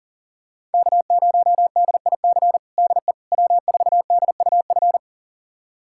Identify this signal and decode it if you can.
Morse code, heard as a keyed tone, about 40 wpm, 700 Hz. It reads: K0DIC DE W4DUF
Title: USA Callsign DE Callsign - 5 Characters - 40wpm